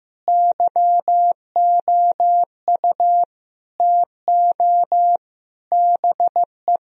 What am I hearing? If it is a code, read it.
Morse code, heard as a keyed tone, about 15 wpm, 700 Hz. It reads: YOU TO BE
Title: Sentences from Top 1000 Words - Review Part 4 - 15wpm